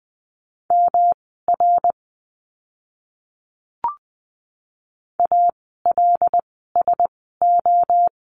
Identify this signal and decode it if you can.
Morse code, heard as a keyed tone, about 20 wpm, spaced as for 14 wpm, 700 Hz. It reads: MR ALSO
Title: 108 Lesson 18f - Letter m - Words 20wpm